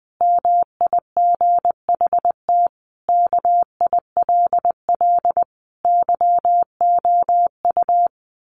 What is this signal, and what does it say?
Morse code, heard as a keyed tone, about 20 wpm, 700 Hz. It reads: MIGHT KILL YOU
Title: A Sound Of Thunder 20wpm